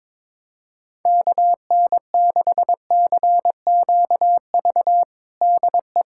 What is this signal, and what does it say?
Morse code, heard as a keyed tone, about 22 wpm, 700 Hz. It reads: KN6CQV DE
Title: USA Callsign DE Callsign - 6 Characters - 22wpm